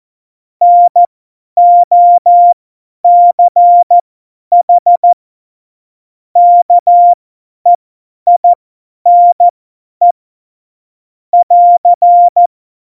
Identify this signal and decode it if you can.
Morse code, heard as a keyed tone, about 14 wpm, spaced as for 10 wpm, 700 Hz. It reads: NOCH KEINE <AR>
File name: Morsekurs / Witze / Witz 17.Dezember 2019, 50 F70